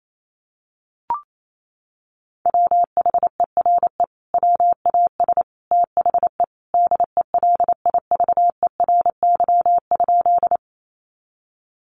Morse code, heard as a keyed tone, about 28 wpm, 700 Hz. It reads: WHERE WAS THE DELIVERY?
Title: Sentences from Top 1500 Words 28wpm